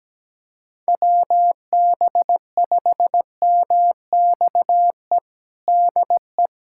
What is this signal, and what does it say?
Morse code, heard as a keyed tone, about 17 wpm, 700 Hz. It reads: WB5MXE DE
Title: USA Callsign DE Callsign - 6 Characters - 17wpm